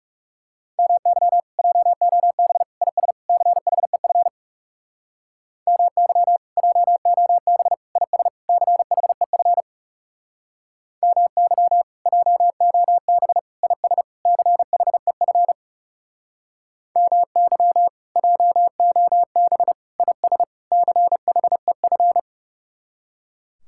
Morse code, heard as a keyed tone, about 38 wpm, 700 Hz. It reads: MY JOB IS CHEF MY JOB IS CHEF MY JOB IS CHEF MY JOB IS CHEF